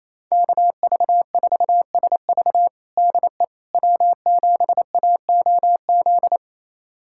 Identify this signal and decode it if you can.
Morse code, heard as a keyed tone, about 28 wpm, 700 Hz. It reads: KV4SV DE W7AOZ
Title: USA Callsign DE Callsign - 5 Characters - 28wpm